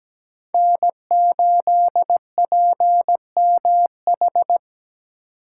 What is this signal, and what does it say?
Morse code, heard as a keyed tone, about 20 wpm, 700 Hz. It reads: N8PMH
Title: Speed Racing - USA 1x3 Call Signs 17wpm